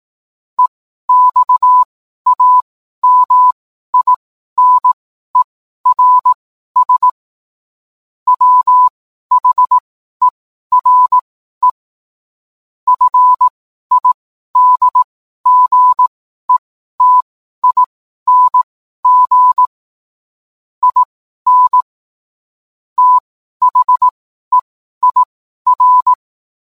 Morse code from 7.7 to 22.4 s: WHERE FIDGETING IN